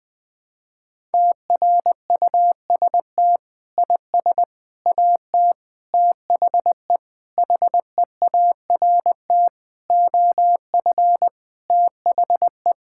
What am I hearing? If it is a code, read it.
Morse code, heard as a keyed tone, about 19 wpm, 700 Hz. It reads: TRUST IS AT THE HEART OF THE
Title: Sentences with 6-Word Phrases - Spoken Beginning - 20wpm